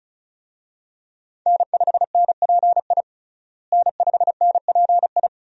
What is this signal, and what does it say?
Morse code, heard as a keyed tone, about 34 wpm, 700 Hz. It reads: N5NPI N5NPI
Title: USA Call Signs - 5 Characters - Repeated 2x - 35wpm